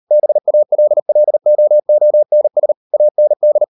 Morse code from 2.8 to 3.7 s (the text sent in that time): AND